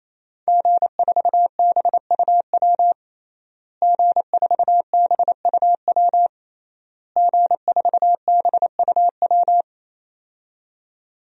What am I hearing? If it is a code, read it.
Morse code, heard as a keyed tone, about 27 wpm, 700 Hz. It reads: G4BUW G4BUW G4BUW
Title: DXCC-100 Call Signs - 5 Characters - Repeated 3x - 28wpm